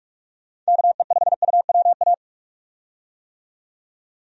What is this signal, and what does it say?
Morse code, heard as a keyed tone, about 48 wpm, 700 Hz. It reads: KE5UWA
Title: USA Call Signs - 6 Characters - 45wpm